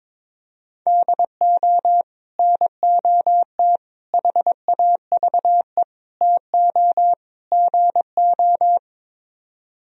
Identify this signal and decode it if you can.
Morse code, heard as a keyed tone, about 23 wpm, 700 Hz. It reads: DO NOT HAVE TO GO
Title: Speed-Racing - Sets of 5 Words 22wpm